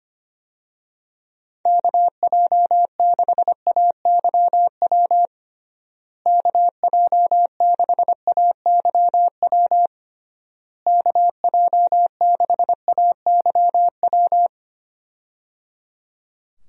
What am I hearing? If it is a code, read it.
Morse code, heard as a keyed tone, about 25 wpm, 700 Hz. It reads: KJ6AYW KJ6AYW KJ6AYW
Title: USA Call Signs - 6 Characters - Repeated 3x - 25wpm